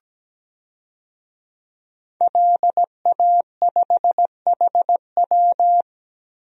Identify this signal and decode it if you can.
Morse code, heard as a keyed tone, about 17 wpm, 700 Hz. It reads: LA5HW
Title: DXCC-100 Call Signs - 5 Characters - Repeated 3x - 17wpm